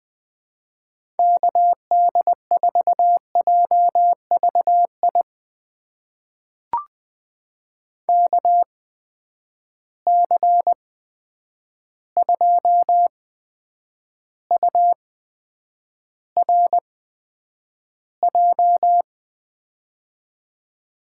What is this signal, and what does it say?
Morse code, heard as a keyed tone, about 20 wpm, spaced as for 9 wpm, 700 Hz. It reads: KD4JVI KC2URJ
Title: USA Call Signs - 6 Characters - 8x Character Spacing - 20wpm